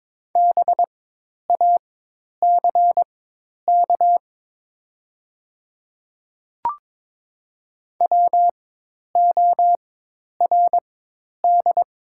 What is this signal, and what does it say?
Morse code, heard as a keyed tone, about 23 wpm, spaced as for 10 wpm, 700 Hz. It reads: BACK WORD
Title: 4x Character Spacing - Top 200 Words - 22wpm